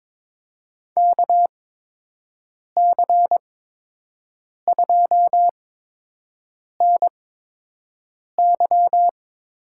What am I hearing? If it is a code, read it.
Morse code, heard as a keyed tone, about 22 wpm, spaced as for 6 wpm, 700 Hz. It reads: KC2NY
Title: USA Call Signs - 6 Characters - 8x Character Spacing - 22wpm